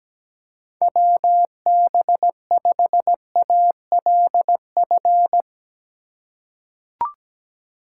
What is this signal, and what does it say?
Morse code, heard as a keyed tone, about 17 wpm, 700 Hz. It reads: WB5ALF
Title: USA Call Signs - 6 Characters - 3x Character Spacing - 17wpm